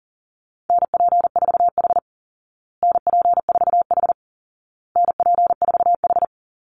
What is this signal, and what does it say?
Morse code, heard as a keyed tone, about 38 wpm, 700 Hz. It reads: NP4H NP4H NP4H
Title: Call Signs Repeated 2x - 40wpm